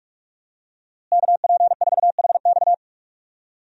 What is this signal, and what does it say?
Morse code, heard as a keyed tone, about 41 wpm, 700 Hz. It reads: KP4HX
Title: USA Call Signs - 5 Characters - Repeated 3x - 45wpm